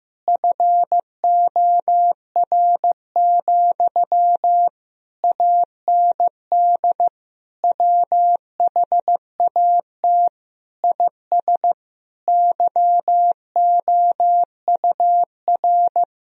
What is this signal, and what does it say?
Morse code, heard as a keyed tone, about 15 wpm, 700 Hz. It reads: FOR, AND WHAT IS YOUR
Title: Sentences from Top 1000 Words - Review Part 4 - 15wpm